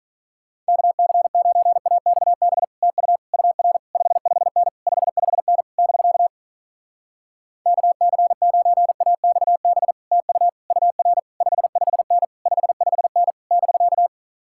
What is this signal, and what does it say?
Morse code, heard as a keyed tone, about 46 wpm, 700 Hz. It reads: KC9AXB TU UR 55N 55N <BK> KC9AXB TU UR 55N 55N <BK>